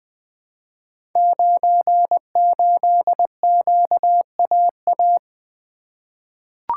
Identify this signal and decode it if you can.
Morse code, heard as a keyed tone, about 20 wpm, 700 Hz. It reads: 98QAA
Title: Code Groups - 5 Letter:Numbers - 20wpm